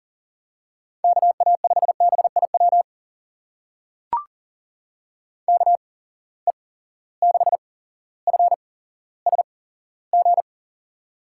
Morse code, heard as a keyed tone, about 41 wpm, spaced as for 17 wpm, 700 Hz. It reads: KA5BIW KE6FSG
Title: USA Call Signs - 6 Characters - 8x Character Spacing - 40wpm